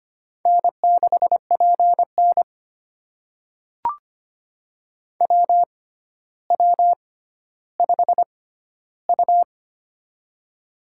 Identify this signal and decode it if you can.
Morse code, heard as a keyed tone, about 25 wpm, spaced as for 13 wpm, 700 Hz. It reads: N6PN WW5U
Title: USA Call Signs - 4 Characters - 6x Character Spacing - 25wpm